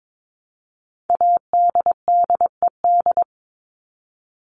Morse code, heard as a keyed tone, about 22 wpm, 700 Hz. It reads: ADDED
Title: ED Suffix - Top 100 Words with Base Words 22wpm